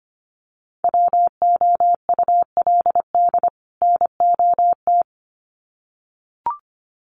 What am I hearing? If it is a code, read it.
Morse code, heard as a keyed tone, about 25 wpm, 700 Hz. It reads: WOULD NOT
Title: Sets of 2 Words 25wpm